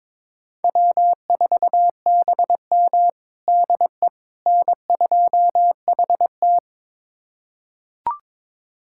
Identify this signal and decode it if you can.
Morse code, heard as a keyed tone, about 22 wpm, 700 Hz. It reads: W4BM DE N2HT
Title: USA Callsign DE Callsign - 4 Characters - 22wpm